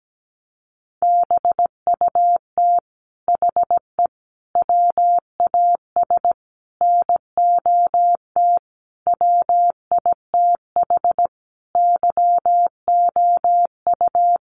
Morse code, heard as a keyed tone, about 17 wpm, 700 Hz. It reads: BUT HE WAS NOT WITH YOU
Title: Sentences from Top 1500 Words 17wpm